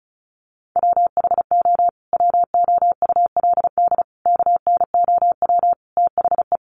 Morse code, heard as a keyed tone, about 34 wpm, 700 Hz. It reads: WHO WOULD KNOW THE
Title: Sentences from Top 800 Words 35wpm